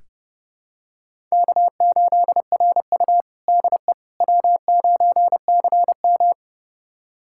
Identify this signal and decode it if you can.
Morse code, heard as a keyed tone, about 30 wpm, 700 Hz. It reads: K8RU DE W9CM
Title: USA Callsign DE Callsign - 4 Characters - 30wpm